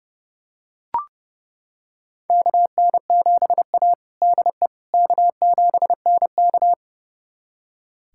Morse code, heard as a keyed tone, about 30 wpm, 700 Hz. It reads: KN7A DE K7NK
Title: USA Callsign DE Callsign - 4 Characters - 30wpm